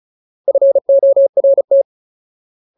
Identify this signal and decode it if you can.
Morse code, heard as a keyed tone, about 34 wpm, 550 Hz. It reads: FORT
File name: Ultimate Word List / 35WPM Ultimate